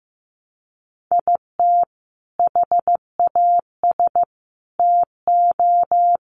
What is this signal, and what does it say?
Morse code, heard as a keyed tone, about 15 wpm, 700 Hz. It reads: IT HAS TO
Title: Sets of 4 Words 15wpm